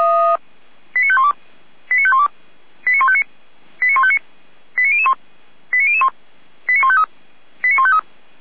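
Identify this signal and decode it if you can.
Five-tone selective call (ZVEI), heard as quick runs of five tones, tones of about 70 ms. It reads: 89512 89512 89159 89159 89001 89001 89153 89153
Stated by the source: ZVEI five-tone selective call received by SDR.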